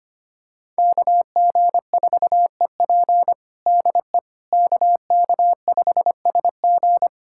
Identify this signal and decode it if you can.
Morse code, heard as a keyed tone, about 25 wpm, 700 Hz. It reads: KG4EP DE KK5SG
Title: USA Callsign DE Callsign - 5 Characters - 25wpm